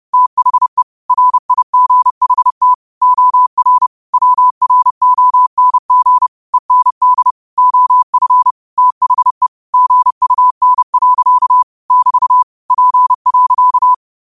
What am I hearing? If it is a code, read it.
Morse code, heard as a keyed tone, about 30 wpm, 1000 Hz. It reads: THE RIGHT OR WRONG END OF THE GUN. <BT> P.